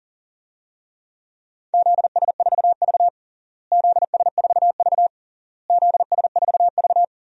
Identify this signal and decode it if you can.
Morse code, heard as a keyed tone, about 40 wpm, 700 Hz. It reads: ZS4V ZS4V ZS4V
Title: DXCC-100 Call Signs - 4 Characters - Repeated 3x - 40wpm